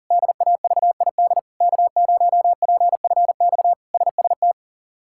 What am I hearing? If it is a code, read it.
Morse code, heard as a keyed tone, about 37 wpm, 700 Hz. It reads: DAVID K0PFX SST
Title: K1USN SST QSO Practice - 40wpm